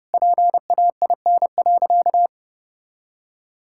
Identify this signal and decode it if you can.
Morse code, heard as a keyed tone, about 32 wpm, 700 Hz. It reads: PAIN.
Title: Sentences from Top 1600 Words 30wpm